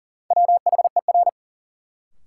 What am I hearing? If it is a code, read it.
Morse code, heard as a keyed tone, about 40 wpm, 700 Hz. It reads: WHER
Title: Sentences from Top 1500 Words - New Word Drill Alt - 40wpm